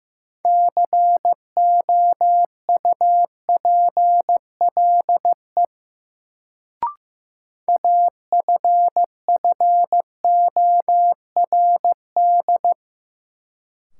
Morse code, heard as a keyed tone, about 15 wpm, 700 Hz. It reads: COUPLE AFFORD
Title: Sentences from Top 1100 Words - New Word Drill Alt - 15wpm